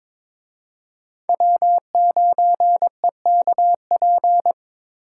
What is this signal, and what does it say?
Morse code, heard as a keyed tone, about 26 wpm, 700 Hz. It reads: W9EKP
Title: Speed Racing - USA 1x3 Call Signs 22wpm